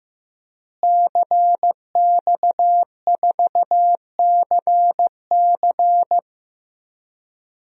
Morse code, heard as a keyed tone, about 16 wpm, 700 Hz. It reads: CX4CC
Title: DXCC-100 Call Signs - 5 Characters - Repeated 2x - 15wpm